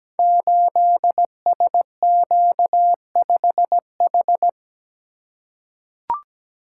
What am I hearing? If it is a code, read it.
Morse code, heard as a keyed tone, about 17 wpm, 700 Hz. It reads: 8SQ5H
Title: Code Groups - 5 Letter:Numbers - 17wpm